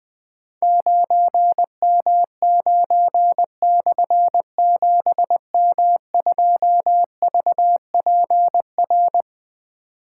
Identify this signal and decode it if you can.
Morse code, heard as a keyed tone, about 20 wpm, 700 Hz. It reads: 9M9/7M2VPR
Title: Difficult Call Signs 20wpm